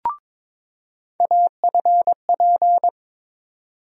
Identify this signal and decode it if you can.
Morse code, heard as a keyed tone, about 22 wpm, 700 Hz. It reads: AFP
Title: USA Call Sign Postfixes - 3 Letters 22wpm